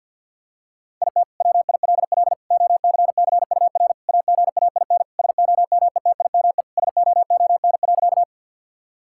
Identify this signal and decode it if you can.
Morse code, heard as a keyed tone, about 50 wpm, 700 Hz. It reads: IT WILL OCCUR AGAIN SOMETIME SOON.
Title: Sentences from Top 1000 Words - Review Part 6 - 50wpm